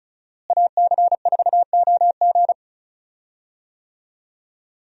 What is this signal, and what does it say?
Morse code, heard as a keyed tone, about 36 wpm, 700 Hz. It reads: AC4OG
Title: USA Call Signs - 5 Characters - Repeated 3x - 35wpm